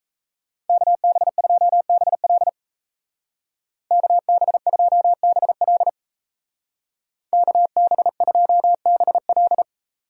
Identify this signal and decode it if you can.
Morse code, heard as a keyed tone, about 33 wpm, 700 Hz. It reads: KB2BL KB2BL KB2BL
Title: Speed Racing - USA Call Signs 28wpm